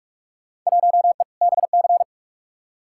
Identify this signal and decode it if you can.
Morse code, heard as a keyed tone, about 41 wpm, 700 Hz. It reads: 1E BC
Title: Field Day Exchanges - 45wpm